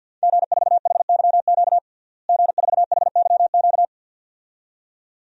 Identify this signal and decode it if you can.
Morse code, heard as a keyed tone, about 52 wpm, 700 Hz. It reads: G4HYQ G4HYQ
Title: DXCC-100 Call Signs - 5 Characters - Repeated 2x - 50wpm